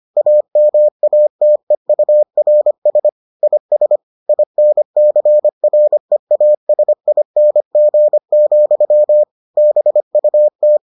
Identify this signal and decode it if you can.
Morse code, heard as a keyed tone, about 25 wpm, 600 Hz. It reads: AMATEURS IS INCREASING, BUT